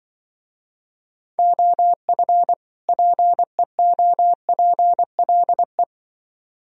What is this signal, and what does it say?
Morse code, heard as a keyed tone, about 24 wpm, 700 Hz. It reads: OF PEOPLE
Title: Speed-Racing - Sets of 2 Words 20wpm